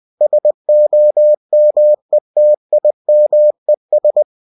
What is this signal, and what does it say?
Morse code, heard as a keyed tone, about 20 wpm, 600 Hz. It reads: SOMETIMES
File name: cw-books / winnie-the-pooh / winnie-20wpm-page-0068